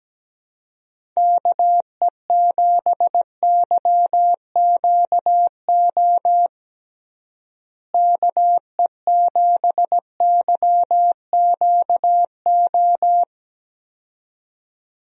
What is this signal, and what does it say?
Morse code, heard as a keyed tone, about 17 wpm, 700 Hz. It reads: KE7YQO KE7YQO
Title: USA Call Signs - 6 Characters - Repeated 2x - 17wpm